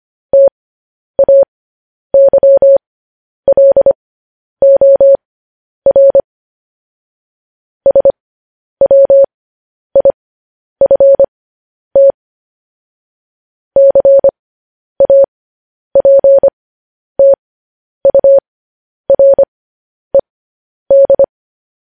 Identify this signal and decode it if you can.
Morse code, heard as a keyed tone, about 25 wpm, spaced as for 10 wpm, 550 Hz. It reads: TAYLOR SWIFT CAPTURED